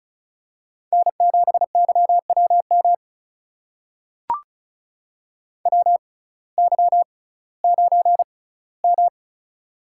Morse code, heard as a keyed tone, about 34 wpm, spaced as for 19 wpm, 700 Hz. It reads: N7YWM WY9M
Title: USA Call Signs - 6x Character Spacing - 35wpm